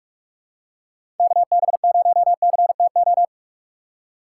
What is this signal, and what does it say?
Morse code, heard as a keyed tone, about 46 wpm, 700 Hz. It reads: KB0CTO
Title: USA Call Signs - 6 Characters - 8x Character Spacing - 45wpm